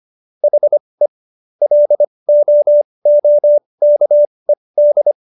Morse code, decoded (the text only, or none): HE LOOKED